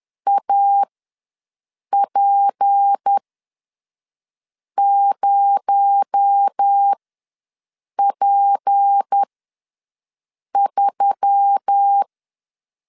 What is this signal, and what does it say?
Morse code, heard as a keyed tone, about 10 wpm, spaced as for 5 wpm, 800 Hz. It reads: AP0P3
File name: LEZIONI / 9 2 ws 10wpm 4130